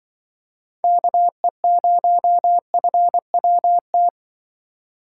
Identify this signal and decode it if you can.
Morse code, heard as a keyed tone, about 24 wpm, 700 Hz. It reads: KE0FWT
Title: Speed Racing - USA 2x3 Call Signs 20wpm